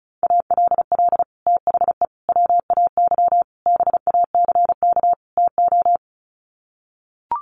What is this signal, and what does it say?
Morse code, heard as a keyed tone, about 34 wpm, 700 Hz. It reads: ALL THE WAY BACK TO
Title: Sets of 5 Words Encore 35wpm